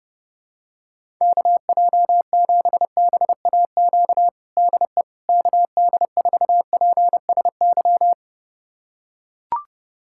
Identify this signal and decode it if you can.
Morse code, heard as a keyed tone, about 30 wpm, 700 Hz. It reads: KJ7BAQ DE KD4PSY
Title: USA Callsign DE Callsign 30wpm